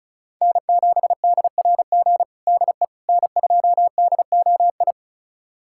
Morse code, heard as a keyed tone, about 36 wpm, 700 Hz. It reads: N7DRG DE N2DOI
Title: USA Callsign DE Callsign - 5 Characters - 35wpm